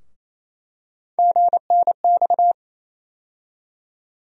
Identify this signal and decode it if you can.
Morse code, heard as a keyed tone, about 27 wpm, 700 Hz. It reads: GNX
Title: USA Call Sign Postfixes - 3 Letters 28wpm